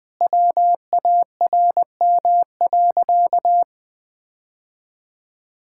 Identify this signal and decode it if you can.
Morse code, heard as a keyed tone, about 20 wpm, 700 Hz. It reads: WARM.
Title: Sentences with 2-Word Phrases - 20wpm